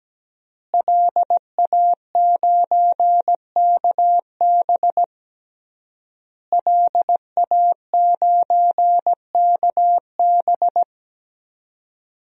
Morse code, heard as a keyed tone, about 17 wpm, 700 Hz. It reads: LA9KB LA9KB
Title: DXCC-100 Call Signs - 5 Characters - Repeated 3x - 17wpm